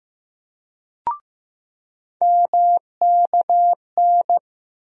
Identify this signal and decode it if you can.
Morse code, heard as a keyed tone, about 15 wpm, 700 Hz. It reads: MKN
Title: USA Call Sign Postfixes - 3 Letters 15wpm